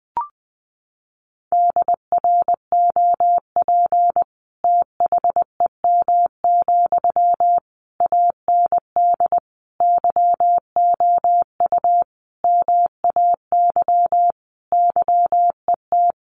Morse code, heard as a keyed tone, about 20 wpm, 700 Hz. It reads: DROP THEM, AND YOU MAY YET
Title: Sentences from Top 1400 Words 20wpm